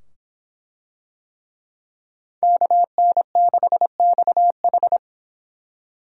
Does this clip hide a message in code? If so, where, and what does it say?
Yes: Morse code from 2.4 to 5.0 s, heard as a keyed tone, about 24 wpm, 700 Hz. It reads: KN6XH